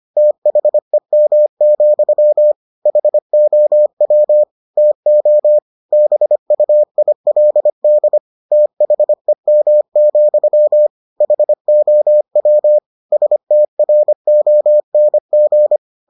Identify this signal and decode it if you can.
Morse code, heard as a keyed tone, about 25 wpm, 600 Hz. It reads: THEM, HOW TO BUILD THEM, HOW STRONG